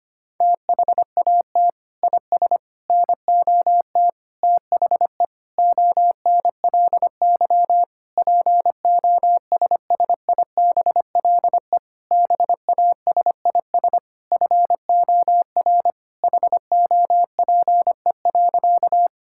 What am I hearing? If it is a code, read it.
Morse code, heard as a keyed tone, about 25 wpm, 700 Hz. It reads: THAT IS NOT THE ONLY POSSIBLE BASIS FOR HOPE.
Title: Sentences from Top 1000 Words - Review Part 5 - 25wpm